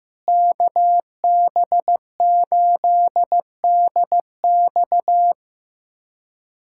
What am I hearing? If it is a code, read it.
Morse code, heard as a keyed tone, about 15 wpm, 700 Hz. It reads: KB8DX
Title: K1USN SST QSO Practice - 15wpm